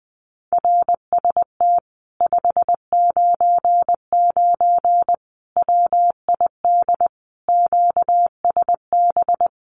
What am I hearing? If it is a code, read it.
Morse code, heard as a keyed tone, about 20 wpm, 700 Hz. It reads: RST 599 WID QSB
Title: Common QSO Phrases Encore 20wpm